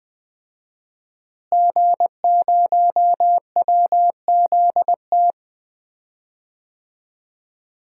Morse code, heard as a keyed tone, about 20 wpm, 700 Hz. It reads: G0WZT
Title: DXCC-100 Call Signs - 5 Characters - Repeated 2x - 20wpm